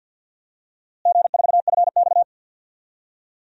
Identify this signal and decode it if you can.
Morse code, heard as a keyed tone, about 49 wpm, 700 Hz. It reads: G4FX
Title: DXCC-100 Call Signs - 4 Characters - Repeated 3x - 50wpm